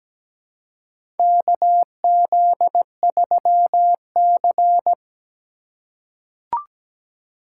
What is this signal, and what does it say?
Morse code, heard as a keyed tone, about 17 wpm, 700 Hz. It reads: KZ3C
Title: USA Call Signs - 4 Characters - 3x Character Spacing - 17wpm